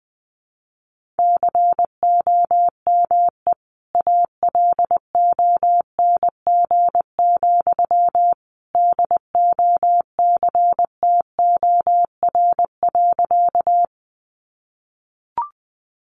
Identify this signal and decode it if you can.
Morse code, heard as a keyed tone, about 20 wpm, 700 Hz. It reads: COME ALONG, DOCTOR.
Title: Sentences from Top 1500 Words 20wpm